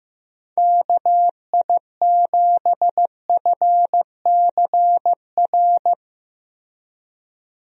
Morse code, heard as a keyed tone, about 15 wpm, 700 Hz. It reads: KI7FCR
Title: USA Call Signs - 6 Characters - Repeated 2x - 15wpm